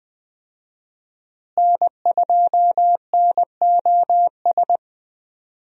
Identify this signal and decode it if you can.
Morse code, heard as a keyed tone, about 20 wpm, 700 Hz. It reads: N2NOS
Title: USA Call Signs - 5 Characters - Repeated 3x - 20wpm